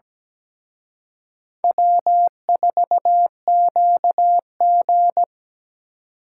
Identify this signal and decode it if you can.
Morse code, heard as a keyed tone, about 20 wpm, 700 Hz. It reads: W4QG
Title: Speed Racing - USA 1x2 Call Signs 17wpm